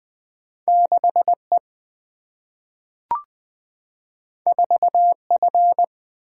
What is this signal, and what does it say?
Morse code, heard as a keyed tone, about 20 wpm, 700 Hz. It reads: 6 E 4 F
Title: Two Letter-Number Combinations - 20wpm